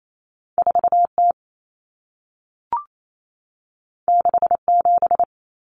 Morse code, heard as a keyed tone, about 28 wpm, 700 Hz. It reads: 4T 67
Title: Common Cut Numbers - 2-Digit 28wpm